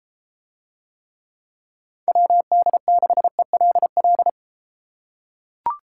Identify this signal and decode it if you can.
Morse code, heard as a keyed tone, about 30 wpm, 700 Hz. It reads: WD6ELL